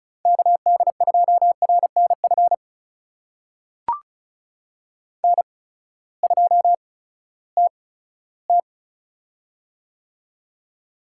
Morse code, heard as a keyed tone, about 34 wpm, spaced as for 17 wpm, 700 Hz. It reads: KD2RNF N2TT
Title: USA Call Signs - 8x Character Spacing - 35wpm